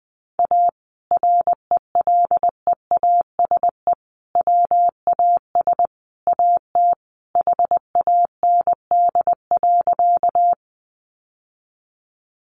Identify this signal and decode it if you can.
Morse code, heard as a keyed tone, about 20 wpm, 700 Hz. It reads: A RELEASE WAS AT HAND.
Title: Sentences from Top 1100 Words 20wpm